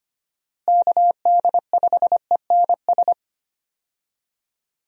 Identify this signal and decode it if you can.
Morse code, heard as a keyed tone, about 25 wpm, 700 Hz. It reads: KD5ENS
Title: USA Call Signs - 6 Characters - Repeated 2x - 25wpm